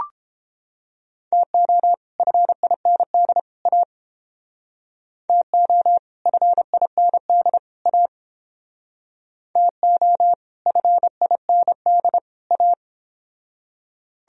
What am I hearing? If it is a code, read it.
Morse code, heard as a keyed tone, about 31 wpm, 700 Hz. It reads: TO FIND A TO FIND A TO FIND A